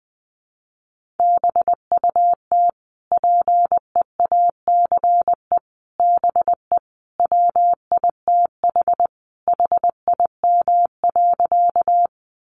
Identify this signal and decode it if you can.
Morse code, heard as a keyed tone, about 20 wpm, 700 Hz. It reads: BUT PEACE BE WITH HIM.
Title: Sentences from Top 1500 Words 20wpm